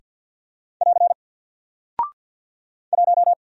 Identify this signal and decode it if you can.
Morse code, heard as a keyed tone, about 48 wpm, 700 Hz. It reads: <AR> 1
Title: Single Character (Letter:Number:Punctuation:Prosign) 50wpm